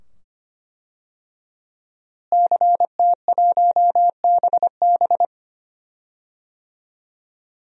Morse code, heard as a keyed tone, about 25 wpm, 700 Hz. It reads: CT1BB
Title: DXCC-100 Call Signs - 5 Characters - Repeated 2x - 25wpm